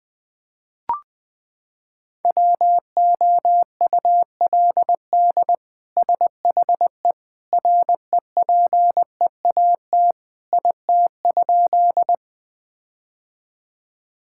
Morse code, heard as a keyed tone, about 21 wpm, 700 Hz. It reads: WOULD SHE REPEAT IT?
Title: Sentences from Top 1600 Words 20wpm